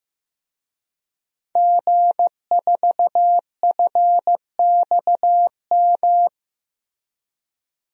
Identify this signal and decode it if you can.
Morse code, heard as a keyed tone, about 15 wpm, 700 Hz. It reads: G4FXM
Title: DXCC-100 Call Signs - 5 Characters - Repeated 2x - 15wpm